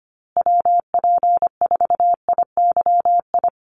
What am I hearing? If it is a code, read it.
Morse code, heard as a keyed tone, about 25 wpm, 700 Hz. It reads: WP4IYI
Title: USA Call Signs Repeated 3x 25wpm